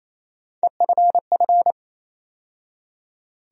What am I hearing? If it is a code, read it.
Morse code, heard as a keyed tone, about 28 wpm, 700 Hz. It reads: EFF
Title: USA Call Sign Postfixes - 3 Letters 28wpm